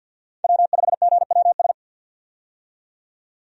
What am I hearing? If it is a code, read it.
Morse code, heard as a keyed tone, about 52 wpm, 700 Hz. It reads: W5GWS
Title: USA Call Signs - 5 Characters - 50wpm